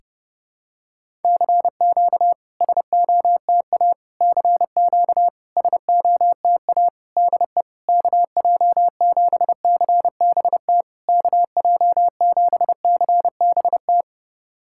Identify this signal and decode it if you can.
Morse code, heard as a keyed tone, about 30 wpm, 690 Hz. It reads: CQ SOTA CQ SOTA DE KJ7CBT KJ7CBT
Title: SOTA QSO Practice 30wpm